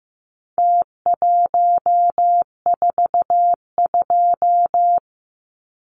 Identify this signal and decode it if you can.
Morse code, heard as a keyed tone, about 15 wpm, 700 Hz. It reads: T142
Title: Numbers - Contest Style - 4-Digit 15wpm